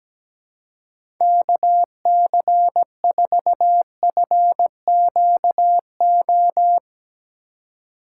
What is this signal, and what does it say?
Morse code, heard as a keyed tone, about 17 wpm, 700 Hz. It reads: KC4FQO
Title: USA Call Signs - 6 Characters - 4x Character Spacing - 17wpm